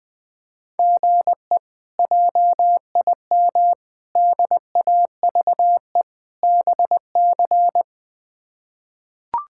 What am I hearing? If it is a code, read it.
Morse code, heard as a keyed tone, about 20 wpm, 700 Hz. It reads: GE JIM DAVE BC
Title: K1USN SST QSO Practice - 20wpm